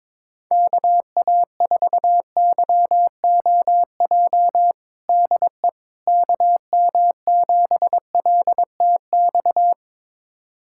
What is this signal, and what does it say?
Morse code, heard as a keyed tone, about 22 wpm, 700 Hz. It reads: KA4YOJ DE KM7LTX
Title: USA Callsign DE Callsign 22wpm